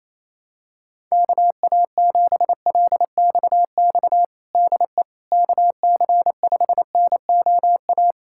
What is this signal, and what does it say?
Morse code, heard as a keyed tone, about 28 wpm, 700 Hz. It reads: KA7LXX DE KC5NOA
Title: USA Callsign DE Callsign 28wpm